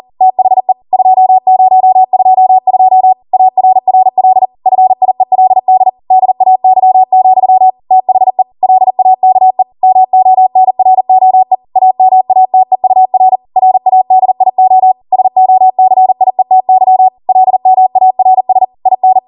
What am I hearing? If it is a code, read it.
Morse code, heard as a keyed tone, about 40 wpm, 750 Hz. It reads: THE 2022 ARRL FIELD DAY, THE LAKE MONROE AMATEUR RADIO SOCIETY LMARS IN